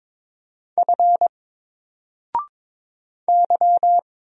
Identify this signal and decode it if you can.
Morse code, heard as a keyed tone, about 22 wpm, 700 Hz. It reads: F Y
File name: Single-Letter-Number-22wpm